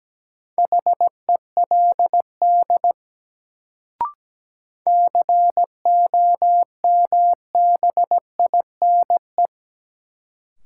Morse code, heard as a keyed tone, about 17 wpm, 700 Hz. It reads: HELD COMBINE
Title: Sentences from Top 1600 Words - New Word Drill Alt - 17wpm